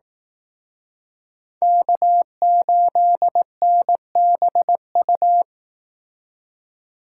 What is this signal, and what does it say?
Morse code, heard as a keyed tone, about 18 wpm, 700 Hz. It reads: K8NBU
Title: Speed Racing - USA 1x3 Call Signs 15wpm